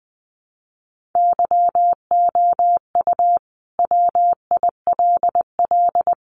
Morse code, heard as a keyed tone, about 20 wpm, 700 Hz. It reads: YOU WILL
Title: Sentences from Top 1400 Words 20wpm